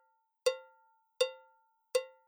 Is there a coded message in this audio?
No.